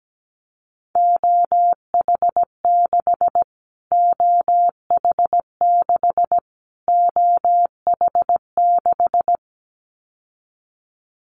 Morse code, heard as a keyed tone, about 17 wpm, 700 Hz. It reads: OH6 OH6 OH6
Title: Callsign Prefixes for Top 25 DXCC - Repeated 3x - 17wpm